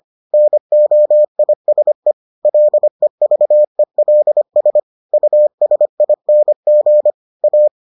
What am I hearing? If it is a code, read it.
Morse code, heard as a keyed tone, about 25 wpm, 600 Hz. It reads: NOISE LEVELS USING A